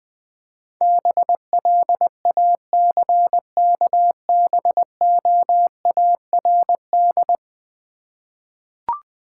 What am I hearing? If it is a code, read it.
Morse code, heard as a keyed tone, about 20 wpm, 700 Hz. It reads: BLACKBOARD
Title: Easy Words with Hints - 20wpm